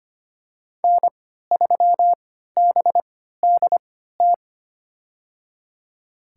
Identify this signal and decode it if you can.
Morse code, heard as a keyed tone, about 25 wpm, spaced as for 14 wpm, 700 Hz. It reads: N3BDT
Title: USA Call Signs - 3x Character Spacing - 25wpm